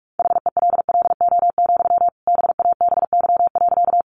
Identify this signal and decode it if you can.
Morse code, heard as a keyed tone, about 44 wpm, 700 Hz. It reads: HELLO, BABY.
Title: Sentences from Top 1300 Words 45wpm